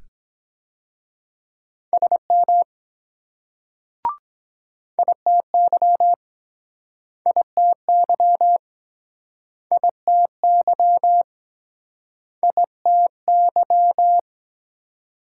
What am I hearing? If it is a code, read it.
Morse code, heard as a keyed tone, about 23 wpm, 700 Hz. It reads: SM ITY ITY ITY ITY